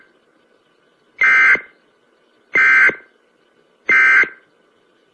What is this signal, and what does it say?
Emergency Alert System (SAME) tones, heard as harsh buzzing bursts. It is heard from 1.2 to 4.4 s.
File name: emergency-broadcast